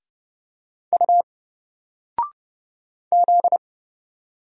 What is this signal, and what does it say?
Morse code, heard as a keyed tone, about 29 wpm, 700 Hz. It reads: U Z
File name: Single-Letter-Number-30wpm